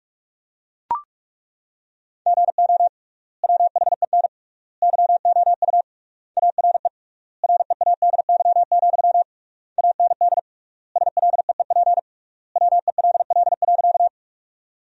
Morse code, heard as a keyed tone, about 42 wpm, 700 Hz. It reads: GO WHEN YOU ARE READY, AND SLEEP WELL.
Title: 3x Word-Spacing - Sentences from Top 500 Words - 45wpm